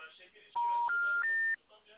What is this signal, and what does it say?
Special information tone, with speech.